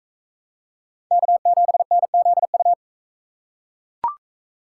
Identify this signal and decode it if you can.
Morse code, heard as a keyed tone, about 40 wpm, 700 Hz. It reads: K7NZU